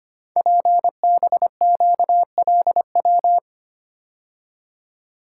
Morse code, heard as a keyed tone, about 25 wpm, 700 Hz. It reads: PBQLW
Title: Code Groups - 5 Letters - 25wpm